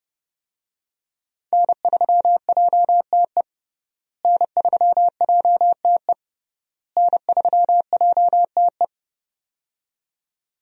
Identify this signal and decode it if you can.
Morse code, heard as a keyed tone, about 32 wpm, 700 Hz. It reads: N3JTE N3JTE N3JTE
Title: USA Call Signs - 5 Characters - Repeated 3x - 30wpm